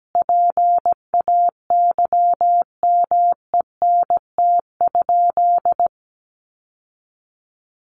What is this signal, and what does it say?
Morse code, heard as a keyed tone, about 17 wpm, 700 Hz. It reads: PAYMENT?
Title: Sentences from Top 1200 Words 17wpm